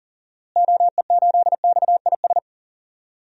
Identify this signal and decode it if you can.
Morse code, heard as a keyed tone, about 40 wpm, 700 Hz. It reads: OE8XIS
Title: DXCC-100 Call Signs - 6 Characters - Repeated 3x - 40wpm